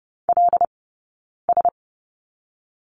Morse code, heard as a keyed tone, about 30 wpm, 700 Hz. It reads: L S
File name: course / 30:10wpm / 086 Lesson 15d - Number 5 - Two Characters 30wpm